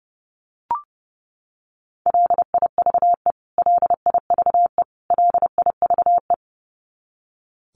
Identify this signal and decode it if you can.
Morse code, heard as a keyed tone, about 30 wpm, 700 Hz. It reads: LIVE LIVE LIVE
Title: Top 300 Words Repeated 30wpm